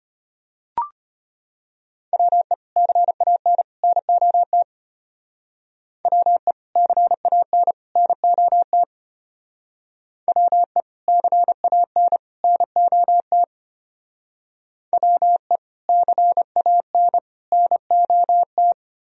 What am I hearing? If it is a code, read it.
Morse code, heard as a keyed tone, about 33 wpm, 700 Hz. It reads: WE CAN NOT WE CAN NOT WE CAN NOT WE CAN NOT